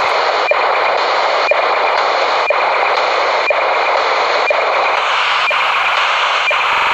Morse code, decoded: none

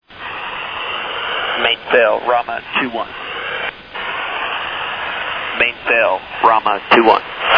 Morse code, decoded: none

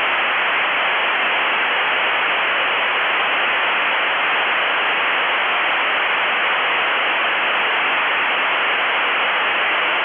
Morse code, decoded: none

